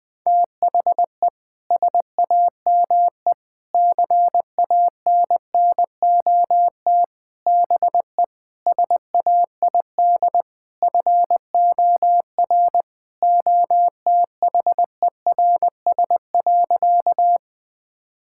Morse code, heard as a keyed tone, about 20 wpm, 700 Hz. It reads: THE SAME CANNOT BE SAID FOR OTHERS.
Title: Sentences with 6-Word Phrases - 20wpm